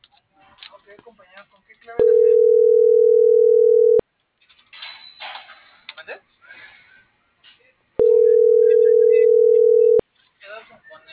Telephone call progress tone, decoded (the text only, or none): ringback tone